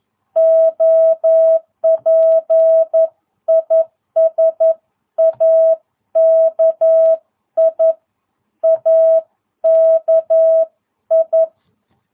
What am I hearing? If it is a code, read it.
Morse code, heard as a keyed tone, about 11 wpm, 650 Hz. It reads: OP IS AKI AKI
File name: jh8cbh / 2022cw